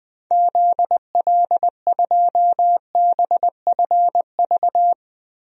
Morse code, heard as a keyed tone, about 20 wpm, 700 Hz. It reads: ZL2BFV
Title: DXCC-100 Call Signs - 6 Characters - Repeated 3x - 20wpm